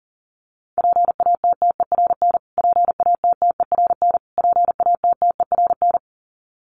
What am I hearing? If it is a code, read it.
Morse code, heard as a keyed tone, about 40 wpm, 700 Hz. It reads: PATTERN PATTERN PATTERN
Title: Top 500 Words Repeated 40wpm